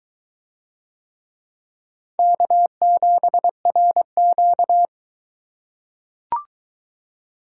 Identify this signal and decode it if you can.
Morse code, heard as a keyed tone, about 22 wpm, 700 Hz. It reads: K7RQ